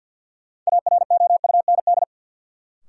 Morse code, heard as a keyed tone, about 53 wpm, 700 Hz. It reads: AROUND
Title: Sets of 2 Words - New Word Drill - 50wpm